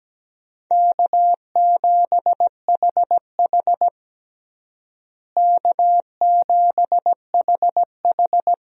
Morse code, heard as a keyed tone, about 17 wpm, 700 Hz. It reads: K7HH K7HH
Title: USA Call Signs - 4 Characters - Repeated 2x - 17wpm